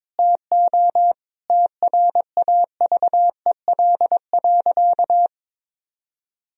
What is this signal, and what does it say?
Morse code, heard as a keyed tone, about 21 wpm, 700 Hz. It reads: TO TRAVEL.
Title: Sentences from Top 1000 Words - Review Part 2 - 22wpm